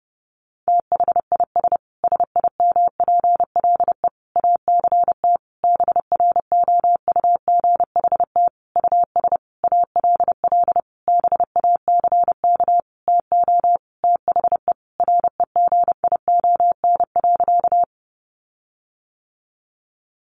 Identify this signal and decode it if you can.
Morse code, heard as a keyed tone, about 28 wpm, 700 Hz. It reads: THIS SIMPLE ACT BROUGHT US ALL BACK TO THE REGION.
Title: Sentences from Top 1500 Words 30wpm